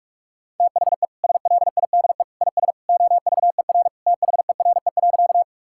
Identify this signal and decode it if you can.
Morse code, heard as a keyed tone, about 47 wpm, 700 Hz. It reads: THE SLIDE IS OVER THERE.
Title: Sentences from Top 1600 Words 45wpm